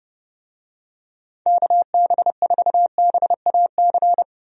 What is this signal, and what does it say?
Morse code, heard as a keyed tone, about 30 wpm, 700 Hz. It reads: KB4BAC
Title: USA Call Signs - 6 Characters - 30wpm